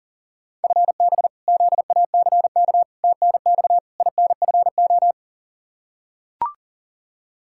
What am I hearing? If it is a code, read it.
Morse code, heard as a keyed tone, about 40 wpm, 700 Hz. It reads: FB ZACK TNX INFO
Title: Practice QSO - Exchange 3 - More Info - 40wpm